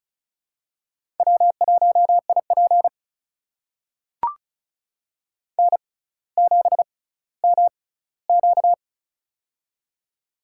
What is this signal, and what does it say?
Morse code, heard as a keyed tone, about 34 wpm, spaced as for 18 wpm, 700 Hz. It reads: W1IP N7MQ
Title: USA Call Signs - 4 Characters - 6x Character Spacing - 35wpm